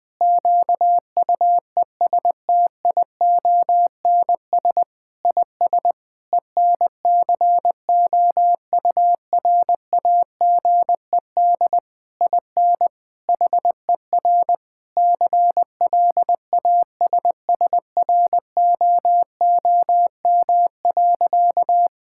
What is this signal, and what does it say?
Morse code, heard as a keyed tone, about 20 wpm, 700 Hz. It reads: QUESTIONS IS ENCOURAGED IN HER CLASSROOM.
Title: Difficult Sentences with Hints - 20wpm